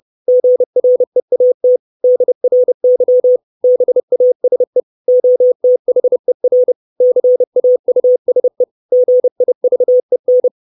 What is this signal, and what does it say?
Morse code, heard as a keyed tone, about 30 wpm, 500 Hz. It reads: GREAT DRY BASE OTHER CAUSE GIVEN
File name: 632 Words 30WPM